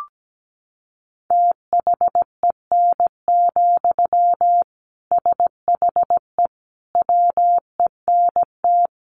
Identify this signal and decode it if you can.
Morse code, heard as a keyed tone, about 17 wpm, 700 Hz. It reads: THEN, SHE WENT
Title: Sentences from Top 1500 Words 17wpm